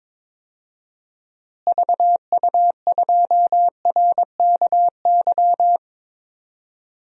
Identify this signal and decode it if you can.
Morse code, heard as a keyed tone, about 22 wpm, 700 Hz. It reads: VU2RKY
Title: DXCC-100 Call Signs - 6 Characters - Repeated 3x - 22wpm